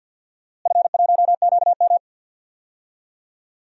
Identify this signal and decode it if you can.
Morse code, heard as a keyed tone, about 46 wpm, 700 Hz. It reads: F1QM
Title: DXCC-100 Call Signs - 4 Characters - Repeated 2x - 50wpm